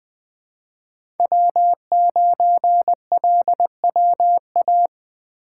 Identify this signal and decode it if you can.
Morse code, heard as a keyed tone, about 20 wpm, 700 Hz. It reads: W9LWA
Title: USA Call Signs - 2x Character Spacing - 20wpm